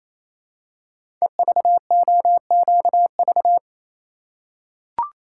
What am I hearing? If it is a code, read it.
Morse code, heard as a keyed tone, about 27 wpm, 700 Hz. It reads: EVOQV
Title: Code Groups - 5 Letters - 28wpm